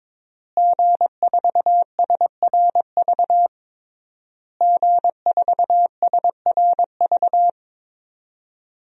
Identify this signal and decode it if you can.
Morse code, heard as a keyed tone, about 22 wpm, 700 Hz. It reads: G4SRV G4SRV
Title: DXCC-100 Call Signs - 5 Characters - Repeated 3x - 22wpm